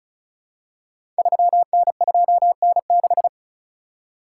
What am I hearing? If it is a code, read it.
Morse code, heard as a keyed tone, about 36 wpm, 690 Hz. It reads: 3N2N6
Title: Code Groups - 5 Letter:Numbers - 35wpm